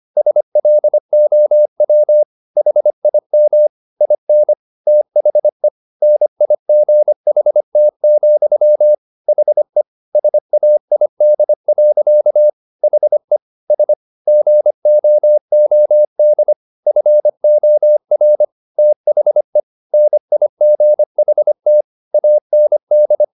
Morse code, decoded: SLOW HIM IN THE NIGHT, HE SAID. HE S GOOD FOR THE NIGHT AND

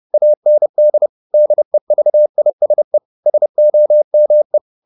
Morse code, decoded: AND DEVISE SOME

